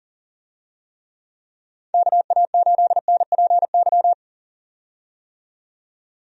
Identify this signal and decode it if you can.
Morse code, heard as a keyed tone, about 40 wpm, 700 Hz. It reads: KA8NPY
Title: USA Call Signs - 6 Characters - 40wpm